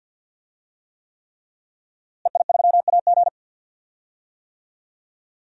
Morse code, heard as a keyed tone, about 51 wpm, 700 Hz. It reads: EI3AG
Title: DXCC-100 Call Signs - 5 Characters - Repeated 3x - 50wpm